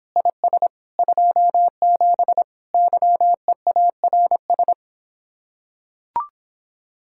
Morse code, heard as a keyed tone, about 26 wpm, 700 Hz. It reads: IS 27 YEARS